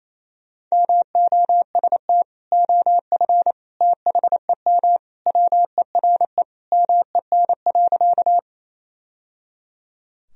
Morse code, heard as a keyed tone, about 28 wpm, 700 Hz. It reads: MOST OF THEM WERE MEN.
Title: Sentences from Top 1000 Words - Review Part 1 - 28wpm